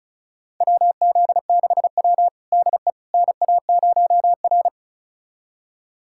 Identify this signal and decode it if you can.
Morse code, heard as a keyed tone, about 36 wpm, 700 Hz. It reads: WZ6W DE NA0R
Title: USA Callsign DE Callsign - 4 Characters - 35wpm